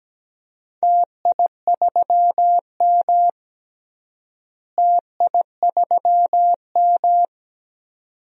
Morse code, heard as a keyed tone, about 17 wpm, 700 Hz. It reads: TI3M TI3M
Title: DXCC-100 Call Signs - 4 Characters - Repeated 2x - 17wpm